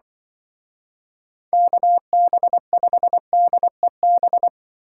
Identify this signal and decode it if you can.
Morse code, heard as a keyed tone, about 25 wpm, 700 Hz. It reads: KB5DEB
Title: Speed Racing - USA Call Signs 20wpm